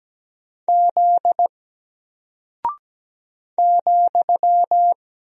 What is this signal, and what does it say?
Morse code, heard as a keyed tone, about 17 wpm, 700 Hz. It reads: Z ,
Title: Single Character (Letter:Number:Punctuation:Prosign) Alt 17wpm